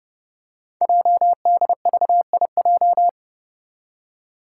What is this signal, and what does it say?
Morse code, heard as a keyed tone, about 30 wpm, 700 Hz. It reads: JDVIJ
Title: Code Groups - 5 Letter:Numbers - 30wpm